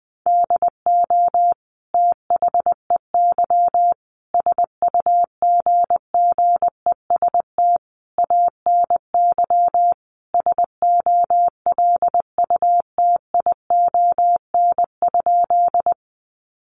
Morse code, heard as a keyed tone, about 19 wpm, 700 Hz. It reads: DO THEY SUGGEST ANY SOLUTION?
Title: Sentences from Top 1300 Words 20wpm